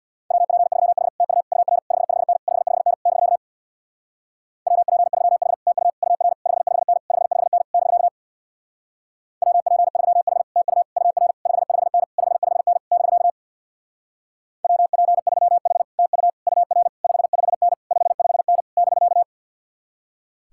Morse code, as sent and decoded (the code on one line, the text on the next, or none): .-- .--. ...-- .... / - ..- / ..- .-. / ..... ..... -. / ..... ..... -. / -...-.- / .-- .--. ...-- .... / - ..- / ..- .-. / ..... ..... -. / ..... ..... -. / -...-.- / .-- .--. ...-- .... / - ..- / ..- .-. / ..... ..... -. / ..... ..... -. / -...-.- / .-- .--. ...-- .... / - ..- / ..- .-. / ..... ..... -. / ..... ..... -. / -...-.-
WP3H TU UR 55N 55N <BK> WP3H TU UR 55N 55N <BK> WP3H TU UR 55N 55N <BK> WP3H TU UR 55N 55N <BK>